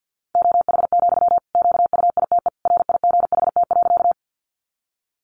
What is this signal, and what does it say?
Morse code, heard as a keyed tone, about 48 wpm, 700 Hz. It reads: OH, QUITE RIGHT.
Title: Sentences from Top 700 Words 50wpm